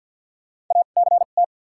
Morse code, heard as a keyed tone, about 53 wpm, spaced as for 37 wpm, 700 Hz. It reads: ACT
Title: 2x Character Spacing - Top 200 Words - 50wpm